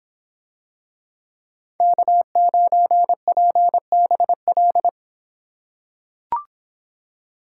Morse code, heard as a keyed tone, about 26 wpm, 700 Hz. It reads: K9PBL